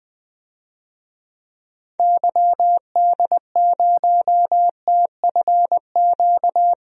Morse code, heard as a keyed tone, about 20 wpm, 700 Hz. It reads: YD0TFQ
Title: DXCC-100 Call Signs - 6 Characters - Repeated 3x - 20wpm